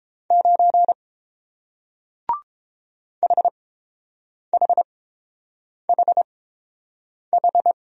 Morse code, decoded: 9 HHHH

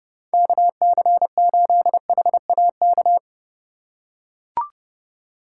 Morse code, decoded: KC8HAK

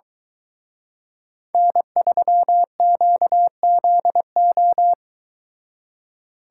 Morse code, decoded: N3QZO

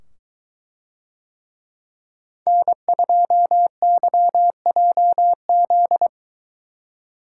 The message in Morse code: N2YJZ

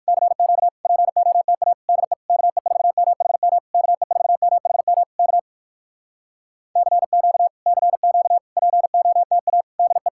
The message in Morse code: CQ POTA DE KE4MHM KE4MHM K CQ CQ POTA DE

